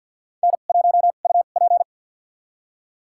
Morse code, heard as a keyed tone, about 52 wpm, spaced as for 36 wpm, 700 Hz. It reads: N1UP
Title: USA Call Signs - 4 Characters - 2x Character Spacing - 50wpm